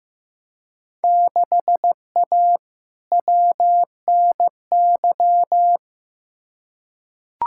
Morse code, heard as a keyed tone, about 15 wpm, 700 Hz. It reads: 6A WNY
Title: Field Day Exchanges - 15wpm